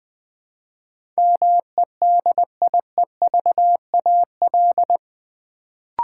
Morse code, heard as a keyed tone, about 20 wpm, 700 Hz. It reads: MEDIEVAL
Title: Build A Word - 3 Letters At A Time - 8-Letter Words - 20wpm